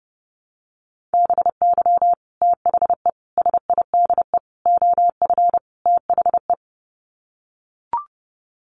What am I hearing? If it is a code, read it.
Morse code, heard as a keyed tone, about 30 wpm, 700 Hz. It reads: BY THE SIDE OF THE
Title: Sets of 5 Words Encore2 30wpm